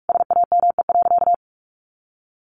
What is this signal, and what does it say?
Morse code, heard as a keyed tone, about 44 wpm, 700 Hz. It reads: SAME.
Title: Sentences from Top 1400 Words 45wpm